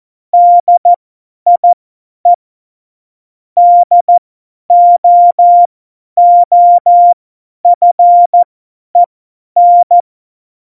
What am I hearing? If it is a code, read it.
Morse code, heard as a keyed tone, about 14 wpm, spaced as for 10 wpm, 700 Hz. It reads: DIE DOOFEN
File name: Morsekurs / Witze / Witz 08. Januar 2019, 50 F70